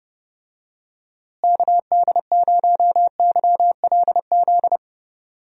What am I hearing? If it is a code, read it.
Morse code, heard as a keyed tone, about 32 wpm, 700 Hz. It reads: KD0YLZ
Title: USA Call Signs - 6 Characters - Repeated 2x - 30wpm